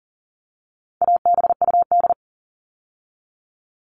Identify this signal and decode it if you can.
Morse code, heard as a keyed tone, about 40 wpm, 700 Hz. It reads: ABUD
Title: All Cut Numbers - 4-Digit 40wpm